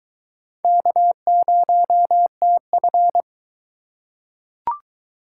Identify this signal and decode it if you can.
Morse code, heard as a keyed tone, about 22 wpm, 700 Hz. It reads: K0TF